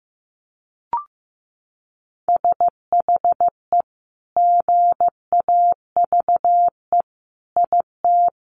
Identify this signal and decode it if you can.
Morse code, heard as a keyed tone, about 15 wpm, 700 Hz. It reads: SHE GAVE IT
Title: Sentences from Top 1500 Words 15wpm